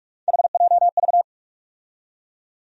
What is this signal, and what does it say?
Morse code, heard as a keyed tone, about 46 wpm, 700 Hz. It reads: HJV
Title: USA Call Sign Postfixes - 3 Letters 45wpm